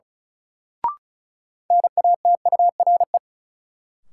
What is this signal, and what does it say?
Morse code, heard as a keyed tone, about 36 wpm, 700 Hz. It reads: NATURE
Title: Sets of 3 Words - New Word Drill - 35wpm